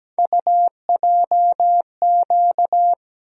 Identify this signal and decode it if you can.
Morse code, heard as a keyed tone, about 17 wpm, 700 Hz. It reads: UJQ
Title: USA Call Sign Postfixes - 3 Letters 17wpm